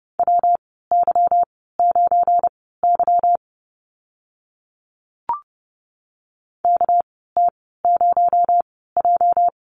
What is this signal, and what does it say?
Morse code, heard as a keyed tone, about 30 wpm, spaced as for 17 wpm, 700 Hz. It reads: WY9Y KT0J
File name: Lesson 37g - Letter x - Call Signs 30wpm